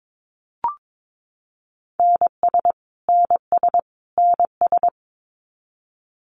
Morse code, heard as a keyed tone, about 22 wpm, 700 Hz. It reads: NS NS NS
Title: Canadian Provinces and Territories - Repeated 3x - 22wpm